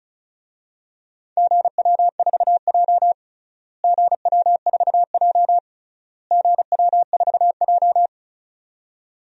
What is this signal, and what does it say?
Morse code, heard as a keyed tone, about 36 wpm, 700 Hz. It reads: GW4J GW4J GW4J
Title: DXCC-100 Call Signs - 4 Characters - Repeated 3x - 35wpm